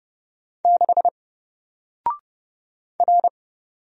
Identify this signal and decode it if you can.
Morse code, heard as a keyed tone, about 30 wpm, 700 Hz. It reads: 6 R
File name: Single-Letter-Number-30wpm